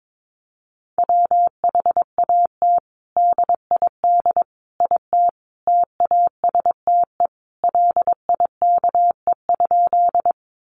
Morse code, heard as a keyed tone, about 22 wpm, 700 Hz. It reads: WHAT DID IT TASTE LIKE?
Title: Sentences from Top 1100 Words 22wpm